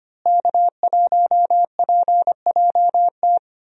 Morse code, heard as a keyed tone, about 25 wpm, 700 Hz. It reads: K1PJT
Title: USA Call Signs - 5 Characters - Repeated 3x - 25wpm